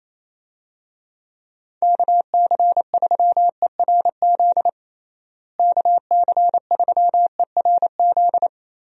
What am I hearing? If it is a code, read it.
Morse code, heard as a keyed tone, about 28 wpm, 700 Hz. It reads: KC3ERZ KC3ERZ
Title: USA Call Signs - 6 Characters - Repeated 3x - 28wpm